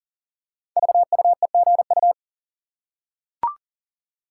Morse code, heard as a keyed tone, about 38 wpm, 690 Hz. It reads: VUEGU
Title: Code Groups - 5 Letters - 40wpm